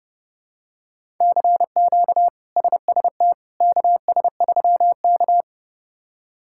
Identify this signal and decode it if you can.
Morse code, heard as a keyed tone, about 30 wpm, 700 Hz. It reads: CQ SST KS3K
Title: K1USN SST QSO Practice - 30wpm